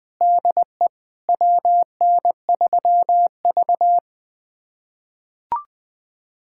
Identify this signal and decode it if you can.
Morse code, heard as a keyed tone, about 20 wpm, 700 Hz. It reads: DE WN3V
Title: USA Callsign DE Callsign - 4 Characters - 20wpm